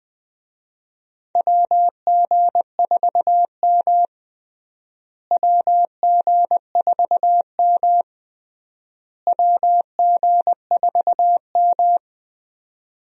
Morse code, heard as a keyed tone, about 20 wpm, 700 Hz. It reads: WG4M WG4M WG4M
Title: USA Call Signs - 4 Characters - Repeated 3x - 20wpm